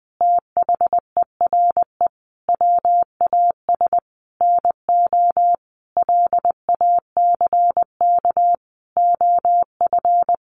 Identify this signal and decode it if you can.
Morse code, heard as a keyed tone, about 20 wpm, 700 Hz. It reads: THERE WAS NO LACK OF
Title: Sentences from Top 1300 Words 20wpm